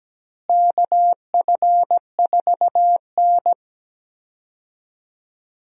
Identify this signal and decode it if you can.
Morse code, heard as a keyed tone, about 17 wpm, 700 Hz. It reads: KF4N
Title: USA Call Signs - 4 Characters - Repeated 2x - 17wpm